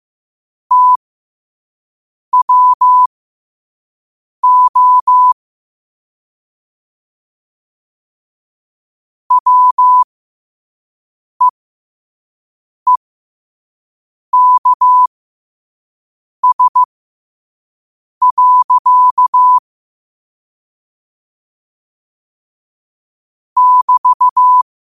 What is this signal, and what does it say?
Morse code, heard as a keyed tone, about 15 wpm, spaced as for 5 wpm, 1000 Hz. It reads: TWO WEEKS. <BT>
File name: qotdfiles / 5wpm